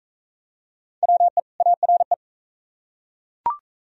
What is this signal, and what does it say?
Morse code, heard as a keyed tone, about 40 wpm, 690 Hz. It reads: WE ARE